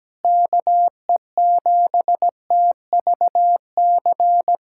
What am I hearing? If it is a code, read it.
Morse code, heard as a keyed tone, about 17 wpm, 700 Hz. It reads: KE7TVC
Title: USA Callsign DE Callsign - 6 Characters - 17wpm